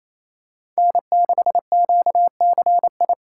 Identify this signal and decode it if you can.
Morse code, heard as a keyed tone, about 28 wpm, 700 Hz. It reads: N6QCI
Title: USA Call Signs - 5 Characters - 6x Character Spacing - 28wpm